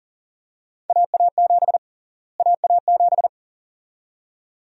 Morse code, heard as a keyed tone, about 40 wpm, 700 Hz. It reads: AA7 AA7
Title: USA Call Sign Prefixes - Repeated 2x - 40wpm